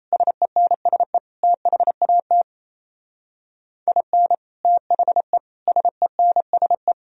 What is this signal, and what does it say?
Morse code, heard as a keyed tone, about 35 wpm, 700 Hz. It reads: SENSE THAT IN THE SENSE